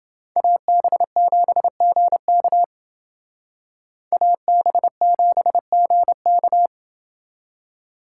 Morse code, heard as a keyed tone, about 28 wpm, 700 Hz. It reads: AB7GK AB7GK